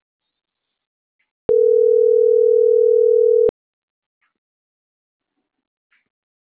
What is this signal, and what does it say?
Ringback tone (North America), 440 + 480 Hz.